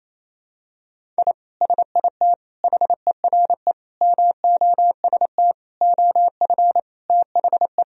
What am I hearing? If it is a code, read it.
Morse code, heard as a keyed tone, about 28 wpm, 700 Hz. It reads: I SIT HERE MOST OF THE
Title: Sentences from Top 1000 Words - Review Part 1 - 28wpm